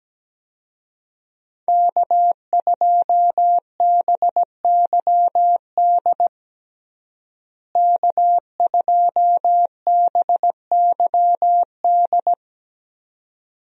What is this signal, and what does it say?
Morse code, heard as a keyed tone, about 17 wpm, 700 Hz. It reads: K2BYD K2BYD
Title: USA Call Signs - 5 Characters - Repeated 2x - 17wpm